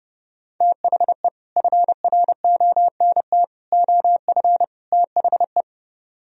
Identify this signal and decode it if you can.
Morse code, heard as a keyed tone, about 30 wpm, 700 Hz. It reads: THE FRONT OF THE